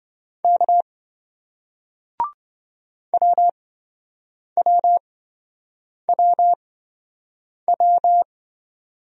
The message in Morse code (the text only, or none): K WWWW